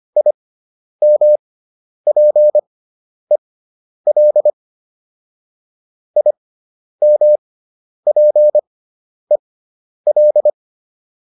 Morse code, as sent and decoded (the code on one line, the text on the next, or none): .. -- .--. .-.. / .. -- .--. .-..
IMPEL IMPEL